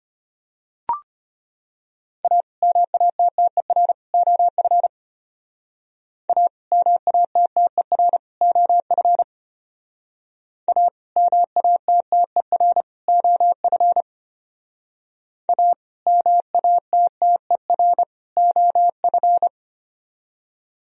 Morse code, heard as a keyed tone, about 33 wpm, 700 Hz. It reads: A MATTER OF A MATTER OF A MATTER OF A MATTER OF